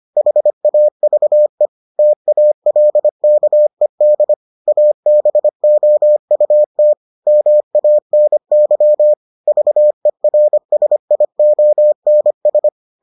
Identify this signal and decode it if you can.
Morse code, heard as a keyed tone, about 25 wpm, 600 Hz. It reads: HAVE TALKED ABOUT MANY VERSIONS